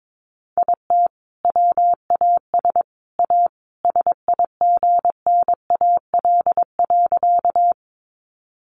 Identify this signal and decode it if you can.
Morse code, heard as a keyed tone, about 22 wpm, 700 Hz. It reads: IT WAS A SIGNAL.
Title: Sentences from Top 1400 Words 22wpm